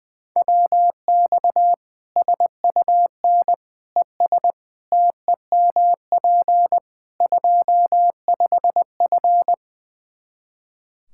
Morse code, heard as a keyed tone, about 21 wpm, 700 Hz. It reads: WX SUN ES TEMP 25F
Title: Practice QSO - Exchange 2 - Info - 20wpm